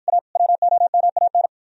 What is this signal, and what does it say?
Morse code, heard as a keyed tone, about 56 wpm, 700 Hz. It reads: A WOMAN